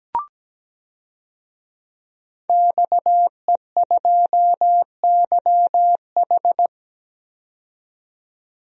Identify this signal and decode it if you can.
Morse code, heard as a keyed tone, about 17 wpm, 700 Hz. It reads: XE2YH
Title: DXCC-100 Call Signs - 5 Characters - 17wpm